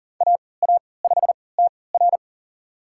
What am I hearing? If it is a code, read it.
Morse code, heard as a keyed tone, about 40 wpm, spaced as for 23 wpm, 700 Hz. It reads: AA5TR
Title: USA Call Signs - 5 Characters - 3x Character Spacing - 40wpm